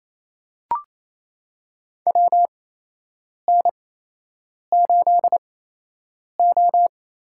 Morse code, heard as a keyed tone, about 27 wpm, spaced as for 8 wpm, 700 Hz. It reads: WN8O
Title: USA Call Signs - 6 Characters - 8x Character Spacing - 28wpm